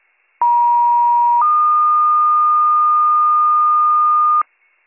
Two-tone page, heard as a one-second tone, then a three-second tone, about 950 then 1220 Hz, with noise. It is heard from 0.4 to 4.4 s.